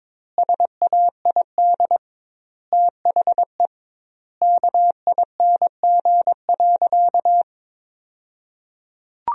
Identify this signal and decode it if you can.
Morse code, heard as a keyed tone, about 22 wpm, 700 Hz. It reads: SAID THE KING.
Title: 2x Word-Spacing - Sentences from Top 400 Words - 22wpm